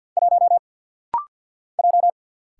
Morse code, heard as a keyed tone, about 48 wpm, 700 Hz. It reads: 1 J
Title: Single Letter-Number - Rapid-Fire 50wpm